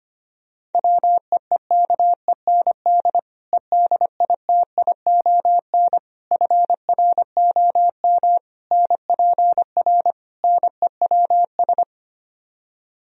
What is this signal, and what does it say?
Morse code, heard as a keyed tone, about 25 wpm, 700 Hz. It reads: WEEKEND EDITION FROM NPR NEWS
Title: Speed-Racing - Sets of 5 Words 25wpm